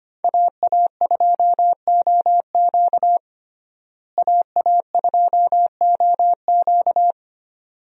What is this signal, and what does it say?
Morse code, heard as a keyed tone, about 25 wpm, 700 Hz. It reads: AA2OQ AA2OQ
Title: USA Call Signs - 5 Characters - Repeated 2x - 25wpm